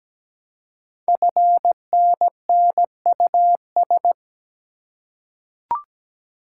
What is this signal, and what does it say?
Morse code, heard as a keyed tone, about 17 wpm, 700 Hz. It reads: FNNUS
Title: Code Groups - 5 Letters - 17wpm